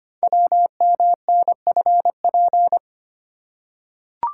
Morse code, heard as a keyed tone, about 26 wpm, 700 Hz. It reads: WMNFP
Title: Code Groups - 5 Letters - 25wpm